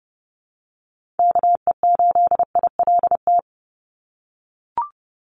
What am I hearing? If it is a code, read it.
Morse code, heard as a keyed tone, about 30 wpm, 700 Hz. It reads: KE8ILT
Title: USA Call Signs 30wpm